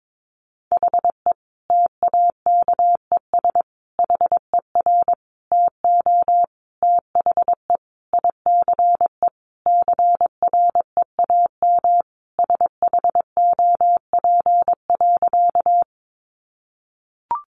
Morse code, heard as a keyed tone, about 22 wpm, 700 Hz. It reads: HE TAKES HER TO THE ICE CREAM SHOP.
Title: Easy Phrases 22wpm